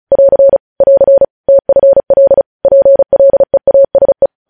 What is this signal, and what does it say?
Morse code, heard as a keyed tone, about 36 wpm, 550 Hz. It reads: <AR> <AR> TFL PLEASE
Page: news headlines in morse code at 35 wpm